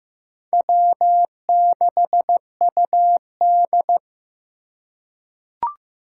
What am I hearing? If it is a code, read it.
Morse code, heard as a keyed tone, about 15 wpm, 700 Hz. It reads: W6UD
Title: USA Call Signs - 4 Characters - 6x Character Spacing - 15wpm